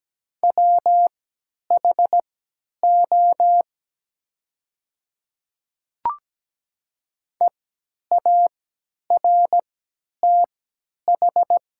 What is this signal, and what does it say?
Morse code, heard as a keyed tone, about 17 wpm, spaced as for 10 wpm, 700 Hz. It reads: WHO EARTH
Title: 3x Character Spacing - Top 200 Words - 17wpm